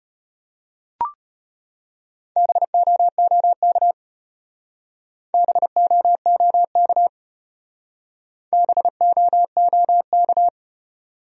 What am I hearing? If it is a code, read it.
Morse code, heard as a keyed tone, about 32 wpm, 700 Hz. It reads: BOOK BOOK BOOK